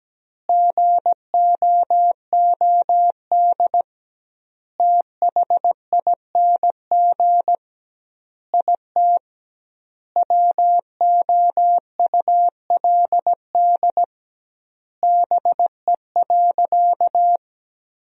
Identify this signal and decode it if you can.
Morse code, heard as a keyed tone, about 17 wpm, 700 Hz. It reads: GOOD THING IT WOULD BE.
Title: 2x Word-Spacing - Sentences from Top 200 Words - 17wpm